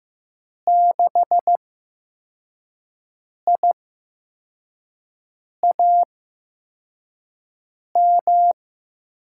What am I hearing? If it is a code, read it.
Morse code, heard as a keyed tone, about 15 wpm, spaced as for 4 wpm, 700 Hz. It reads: 6IAM
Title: USA Call Signs - 8x Character Spacing - 15wpm